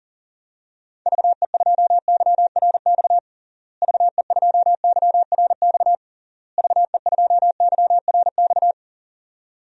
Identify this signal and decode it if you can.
Morse code, heard as a keyed tone, about 40 wpm, 700 Hz. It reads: VE2YRX VE2YRX VE2YRX
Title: DXCC-100 Call Signs - 6 Characters - Repeated 3x - 40wpm